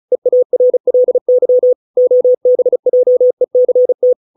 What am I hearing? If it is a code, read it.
Morse code, heard as a keyed tone, about 34 wpm, 500 Hz. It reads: EARLY OBJECT
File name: 632 words / 632 Words 35WPM